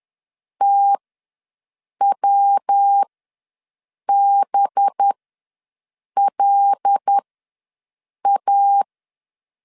Morse code, heard as a keyed tone, about 10 wpm, spaced as for 6 wpm, 800 Hz. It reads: TWBLA
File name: LEZIONI / 5 LBVC / 5 LB ws 10wpm 3616